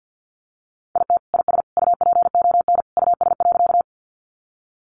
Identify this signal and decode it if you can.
Morse code, heard as a keyed tone, about 50 wpm, 700 Hz. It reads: IT IS UPON US.
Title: Sentences from Top 1100 Words 50wpm